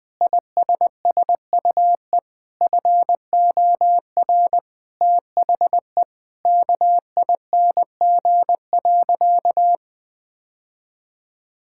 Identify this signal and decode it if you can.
Morse code, heard as a keyed tone, about 21 wpm, 700 Hz. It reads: ISSUE FOR THE KING.
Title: Sentences from Top 1000 Words - Review Part 6 - 20wpm